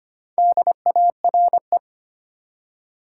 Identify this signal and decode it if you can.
Morse code, heard as a keyed tone, about 25 wpm, 700 Hz. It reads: DARE
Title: Four Letter Words - 25wpm